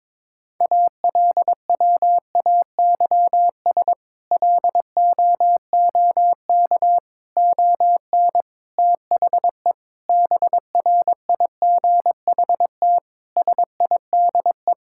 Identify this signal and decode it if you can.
Morse code, heard as a keyed tone, about 21 wpm, 700 Hz. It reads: ALWAYS LOOK ON THE BRIGHT SIDE
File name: practice / 22wpm / Sentences with 8-Word Phrases - 22wpm